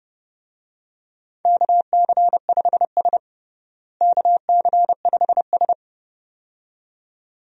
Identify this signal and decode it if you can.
Morse code, heard as a keyed tone, about 32 wpm, 700 Hz. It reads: KC5S KC5S
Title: USA Call Signs - 4 Characters - Repeated 2x - 30wpm